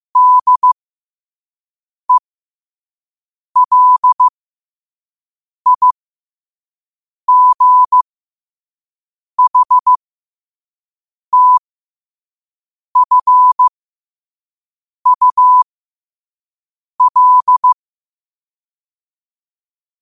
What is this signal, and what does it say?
Morse code, heard as a keyed tone, about 15 wpm, spaced as for 5 wpm, 1000 Hz. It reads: DELIGHTFUL
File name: qotdfiles / 5wpm